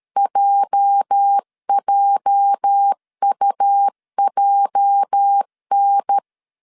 Morse code, heard as a keyed tone, about 13 wpm, 800 Hz. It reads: JJUJN
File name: LEZIONI / 8 YJ45 / 8 YJ 12wpm 2235